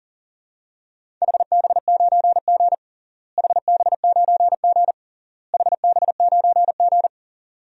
Morse code, heard as a keyed tone, about 40 wpm, 700 Hz. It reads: HB9G HB9G HB9G
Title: DXCC-100 Call Signs - 4 Characters - Repeated 3x - 40wpm